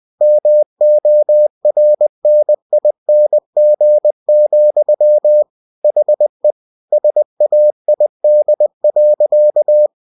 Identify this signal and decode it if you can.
Morse code, heard as a keyed tone, about 20 wpm, 600 Hz. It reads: MORNING, HE SAID.
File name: cw-books / winnie-the-pooh / winnie-20wpm-page-0045